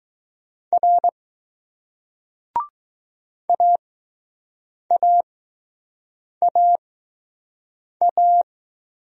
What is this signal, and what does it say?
Morse code, heard as a keyed tone, about 20 wpm, spaced as for 6 wpm, 700 Hz. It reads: R AAAA